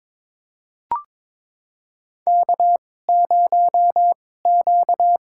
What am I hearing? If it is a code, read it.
Morse code, heard as a keyed tone, about 22 wpm, spaced as for 16 wpm, 700 Hz. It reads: K0Q
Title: USA Call Signs - 2x Character Spacing - 22wpm